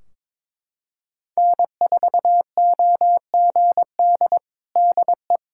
Morse code, heard as a keyed tone, about 22 wpm, 700 Hz. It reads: N4OGD DE
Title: Practice QSO - Exchange 4 - Ending - 22wpm